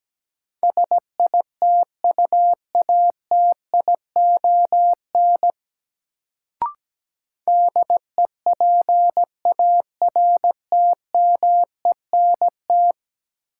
Morse code, heard as a keyed tone, about 17 wpm, 700 Hz. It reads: SITUATION DEPARTMENT
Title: Sentences from Top 1200 Words - New Word Drill Alt - 17wpm